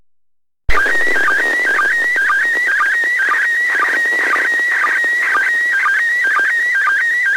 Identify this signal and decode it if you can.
SSTV (slow-scan television), PD 120.
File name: posts / receiving-sstv-images-from-iss / iss sstv2